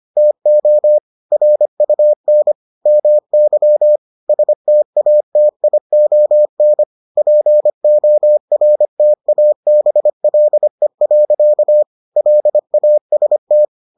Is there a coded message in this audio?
Yes: Morse code from 0.2 to 13.6 s, heard as a keyed tone, about 25 wpm, 600 Hz. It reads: TO RUN MY STATION PORTABLE. LAST